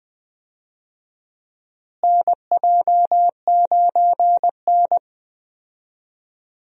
Morse code, heard as a keyed tone, about 20 wpm, 700 Hz. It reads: NJ9N
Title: USA Call Signs - 4 Characters - Repeated 2x - 20wpm